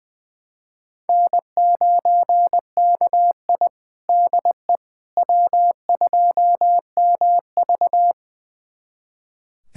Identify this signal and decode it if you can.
Morse code, heard as a keyed tone, about 20 wpm, 700 Hz. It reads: N9KI DE W2MV
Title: USA Callsign DE Callsign - 4 Characters - 20wpm